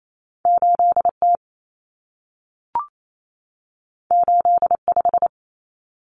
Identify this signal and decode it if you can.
Morse code, heard as a keyed tone, about 28 wpm, 700 Hz. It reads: 8T 85
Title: Common Cut Numbers - 2-Digit 28wpm